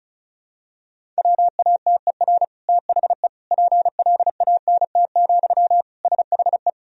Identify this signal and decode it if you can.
Morse code, heard as a keyed tone, about 34 wpm, 700 Hz. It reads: WATER THE PLANT, SHE
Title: Sentences from Top 1000 Words - Review Part 1 - 35wpm